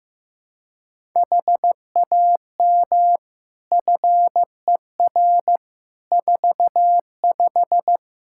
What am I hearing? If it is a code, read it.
Morse code, heard as a keyed tone, about 15 wpm, 700 Hz. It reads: HAM FER 45
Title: Practice QSO - Exchange 3 - More Info - 15wpm